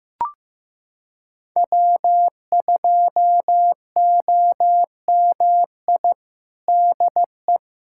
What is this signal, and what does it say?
Morse code, heard as a keyed tone, about 15 wpm, 700 Hz. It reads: W2OMI DE
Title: USA Callsign DE Callsign - 5 Characters - 15wpm